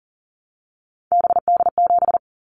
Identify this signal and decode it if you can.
Morse code, heard as a keyed tone, about 38 wpm, 700 Hz. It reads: BD7
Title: Callsign Prefixes for Top 100 DXCC - 40wpm